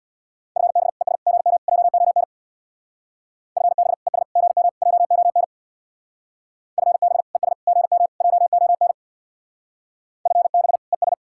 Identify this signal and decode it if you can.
Morse code, heard as a keyed tone, about 63 wpm, 700 Hz. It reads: FB ES GM JON FB ES GM JON FB ES GM JON FB ES